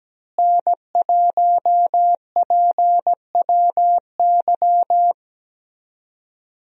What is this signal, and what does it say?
Morse code, heard as a keyed tone, about 17 wpm, 700 Hz. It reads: N1PWY
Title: USA Call Signs - 5 Characters - Repeated 2x - 17wpm